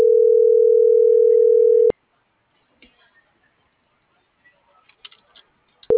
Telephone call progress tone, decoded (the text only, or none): ringback tone